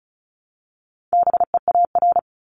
Morse code, heard as a keyed tone, about 36 wpm, 700 Hz. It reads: BEAR
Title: Multi-letter Phonemes Repeated 3x 35wpm